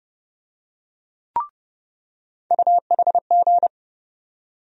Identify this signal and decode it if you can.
Morse code, heard as a keyed tone, about 31 wpm, 700 Hz. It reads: UHG
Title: USA Call Sign Postfixes - 3 Letters 30wpm